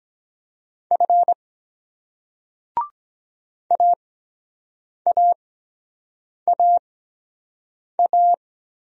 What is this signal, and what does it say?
Morse code, heard as a keyed tone, about 23 wpm, spaced as for 7 wpm, 700 Hz. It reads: F AAAA